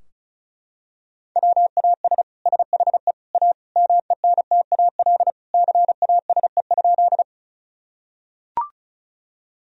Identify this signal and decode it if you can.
Morse code, heard as a keyed tone, about 34 wpm, 700 Hz. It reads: WAS SHE A MENTAL CASE?
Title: Sentences from Top 1000 Words - Review Part 4 - 35wpm